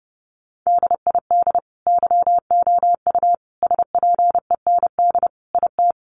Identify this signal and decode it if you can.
Morse code, heard as a keyed tone, about 29 wpm, 700 Hz. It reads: DID YOU SPEND IT
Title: Sentences from Top 800 Words 30wpm